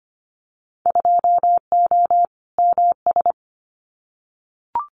Morse code, heard as a keyed tone, about 25 wpm, 700 Hz. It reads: 2O MS
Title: Winter Field Day Exchanges 25wpm